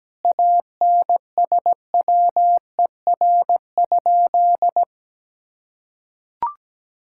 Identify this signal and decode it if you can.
Morse code, heard as a keyed tone, about 17 wpm, 700 Hz. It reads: ANSWER?
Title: Sentences from Top 1000 Words - Review Part 6 - 17wpm